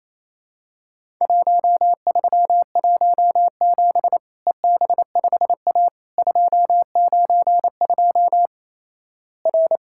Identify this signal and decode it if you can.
Morse code, heard as a keyed tone, about 28 wpm, 700 Hz. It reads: 1317 EB5A 292 R
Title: Worked All Europe DX Contest - QTC Exchange - 28wpm